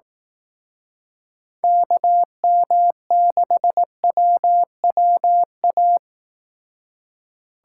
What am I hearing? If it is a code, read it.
Morse code, heard as a keyed tone, about 18 wpm, 700 Hz. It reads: KM6WWA
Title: Speed Racing - USA 2x3 Call Signs 15wpm